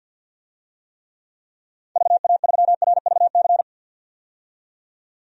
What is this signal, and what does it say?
Morse code, heard as a keyed tone, about 52 wpm, 700 Hz. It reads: VA3RVC
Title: DXCC-100 Call Signs - 6 Characters - 50wpm